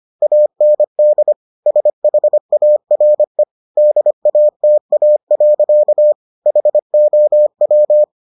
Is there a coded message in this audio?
Yes: Morse code from 0.2 to 8.0 s, heard as a keyed tone, about 25 wpm, 600 Hz. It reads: AND SHARE DATA. HOW